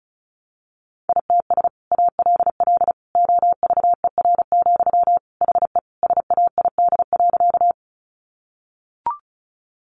Morse code, heard as a keyed tone, about 34 wpm, 700 Hz. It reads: ITS ALL OVER, HE SAID.
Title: Sentences from Top 600 Words 35wpm